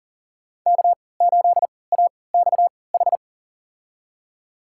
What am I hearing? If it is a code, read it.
Morse code, heard as a keyed tone, about 40 wpm, spaced as for 23 wpm, 700 Hz. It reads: K8AXH
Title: USA Call Signs - 5 Characters - 3x Character Spacing - 40wpm